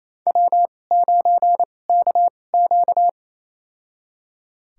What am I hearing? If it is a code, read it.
Morse code, heard as a keyed tone, about 28 wpm, spaced as for 20 wpm, 700 Hz. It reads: W9KQ
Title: USA Call Signs - 2x Character Spacing - 28wpm